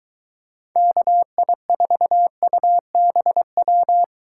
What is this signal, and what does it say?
Morse code, heard as a keyed tone, about 23 wpm, 700 Hz. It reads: KI4UBW